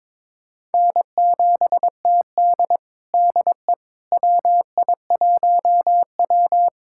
Morse code, heard as a keyed tone, about 22 wpm, 700 Hz. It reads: N7TD DE WI1W
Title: USA Callsign DE Callsign - 4 Characters - 22wpm